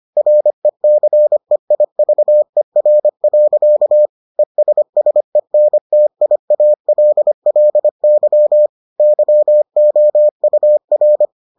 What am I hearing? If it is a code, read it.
Morse code, heard as a keyed tone, about 25 wpm, 600 Hz. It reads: RECEIVER. ESSENTIALLY YOUR